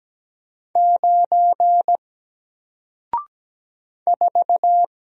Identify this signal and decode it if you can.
Morse code, heard as a keyed tone, about 17 wpm, 700 Hz. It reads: 9 4
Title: Single Character (Letter:Number:Punctuation:Prosign) 17wpm